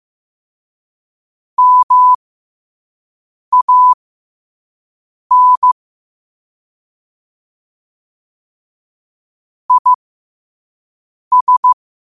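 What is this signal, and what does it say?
Morse code, heard as a keyed tone, about 15 wpm, spaced as for 5 wpm, 1000 Hz. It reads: MAN IS
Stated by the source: Morse practice quote sent at 5 wpm.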